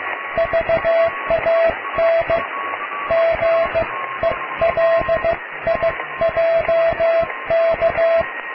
Morse code, heard as a keyed tone, about 15 wpm, 660 Hz, with radio noise. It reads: VAN GELIJK
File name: cw / tekst(a)+QRM